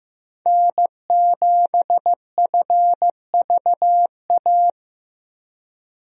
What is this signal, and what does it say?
Morse code, heard as a keyed tone, about 16 wpm, 700 Hz. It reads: N7FVA
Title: USA Call Signs - 5 Characters - Repeated 2x - 15wpm